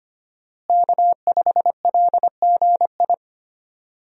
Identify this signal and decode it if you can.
Morse code, heard as a keyed tone, about 25 wpm, 700 Hz. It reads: K5LGI
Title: USA Call Signs - 5 Characters - Repeated 3x - 25wpm